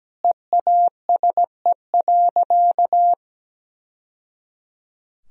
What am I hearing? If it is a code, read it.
Morse code, heard as a keyed tone, about 17 wpm, 700 Hz. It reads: EASE.
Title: Sentences from Top 1000 Words - Review Part 1 - 17wpm